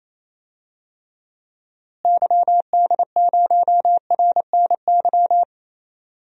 Morse code, heard as a keyed tone, about 28 wpm, 700 Hz. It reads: YD0RNY
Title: DXCC-100 Call Signs - 6 Characters - Repeated 3x - 28wpm